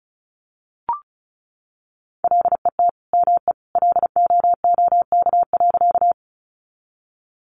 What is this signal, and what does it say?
Morse code, heard as a keyed tone, about 34 wpm, 700 Hz. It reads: LET ME LOOK.
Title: Sentences from Top 300 Words 35wpm